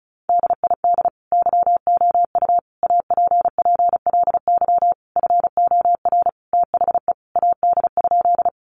Morse code, heard as a keyed tone, about 34 wpm, 700 Hz. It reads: DID YOU APPLY FOR THE AD?
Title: Sentences from Top 1300 Words 35wpm